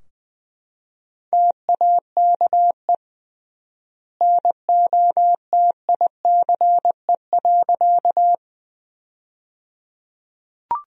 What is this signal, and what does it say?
Morse code, heard as a keyed tone, about 20 wpm, 700 Hz. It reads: TAKE NOTICE.
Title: 3x Word-Spacing - Sentences from Top 400 Words - 20wpm